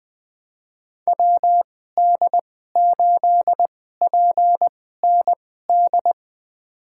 Morse code, heard as a keyed tone, about 20 wpm, spaced as for 14 wpm, 700 Hz. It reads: WD8PND
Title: USA Call Signs - 6 Characters - 2x Character Spacing - 20wpm